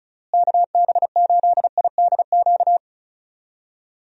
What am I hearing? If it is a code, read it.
Morse code, heard as a keyed tone, about 36 wpm, 700 Hz. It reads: KB8IDQ
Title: USA Call Signs - 6 Characters - Repeated 3x - 35wpm